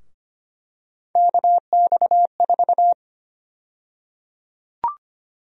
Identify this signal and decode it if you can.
Morse code, heard as a keyed tone, about 25 wpm, 700 Hz. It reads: KX4
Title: USA Call Sign Prefixes - Repeated 2x - 25wpm